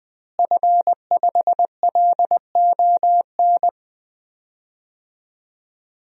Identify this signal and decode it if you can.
Morse code, heard as a keyed tone, about 20 wpm, 700 Hz. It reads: F5LON
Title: DXCC-100 Call Signs - 5 Characters - Repeated 2x - 20wpm